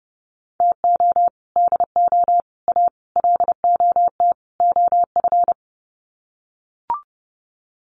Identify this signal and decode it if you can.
Morse code, heard as a keyed tone, about 30 wpm, 700 Hz. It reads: TO DO A LOT OF
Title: Sets of 5 Words Encore 30wpm